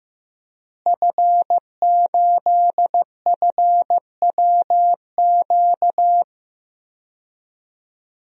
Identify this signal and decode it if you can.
Morse code, heard as a keyed tone, about 15 wpm, 700 Hz. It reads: F8FWQ
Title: DXCC-100 Call Signs - 5 Characters - Repeated 3x - 15wpm